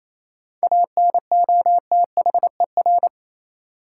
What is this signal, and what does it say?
Morse code, heard as a keyed tone, about 28 wpm, 700 Hz. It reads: ANOTHER
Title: Sets of 5 Words Encore - New Word Drill - 28wpm